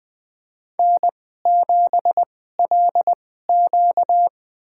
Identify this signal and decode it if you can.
Morse code, heard as a keyed tone, about 20 wpm, spaced as for 15 wpm, 700 Hz. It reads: N7LQ
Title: USA Call Signs - 4 Characters - 2x Character Spacing - 20wpm